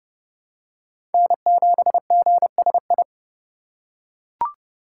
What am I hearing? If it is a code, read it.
Morse code, heard as a keyed tone, about 32 wpm, 700 Hz. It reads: N7GSI
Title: USA Call Signs - 5 Characters - 4x Character Spacing - 30wpm